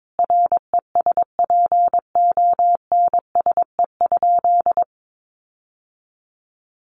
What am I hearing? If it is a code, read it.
Morse code, heard as a keyed tone, about 22 wpm, 700 Hz. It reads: RESPONSE?
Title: Sentences from Top 1400 Words 22wpm